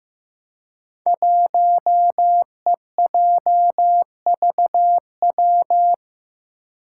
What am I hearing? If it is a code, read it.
Morse code, heard as a keyed tone, about 15 wpm, 700 Hz. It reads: 1EJVW
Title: Five Challenging Letter-Number Combinations 15wpm